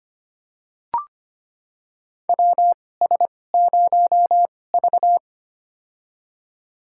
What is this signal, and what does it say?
Morse code, heard as a keyed tone, about 25 wpm, spaced as for 18 wpm, 700 Hz. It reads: WS0V
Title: USA Call Signs - 4 Characters - 2x Character Spacing - 25wpm